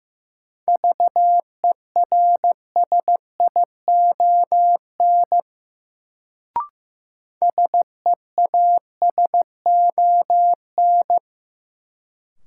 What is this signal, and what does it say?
Morse code, heard as a keyed tone, about 15 wpm, 700 Hz. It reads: VERSION SEASON
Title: Sentences from Top 1400 Words - New Word Drill - 15wpm